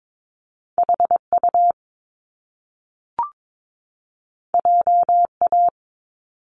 Morse code, heard as a keyed tone, about 22 wpm, 700 Hz. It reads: HU JA
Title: Top Two Letter Combinations in Words 22wpm